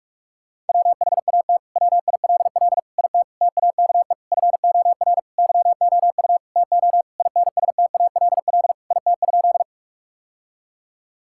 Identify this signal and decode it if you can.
Morse code, heard as a keyed tone, about 46 wpm, 700 Hz. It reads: WHAT WILL IT TAKE FOR YOU TO INSTALL IT?
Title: Sentences from Top 1600 Words 45wpm